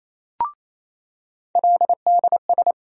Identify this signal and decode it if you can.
Morse code, heard as a keyed tone, about 28 wpm, 700 Hz. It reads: LDS
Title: USA Call Sign Postfixes - 3 Letters 28wpm